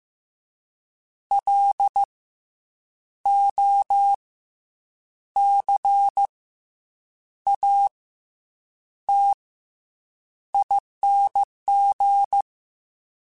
Morse code, heard as a keyed tone, about 15 wpm, 800 Hz. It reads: L O C A T ING